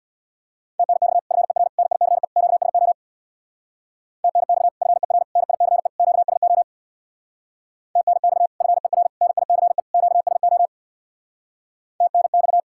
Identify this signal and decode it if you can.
Morse code, heard as a keyed tone, about 64 wpm, 700 Hz. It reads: TNX FER NICE QSO TNX FER NICE QSO TNX FER NICE QSO TNX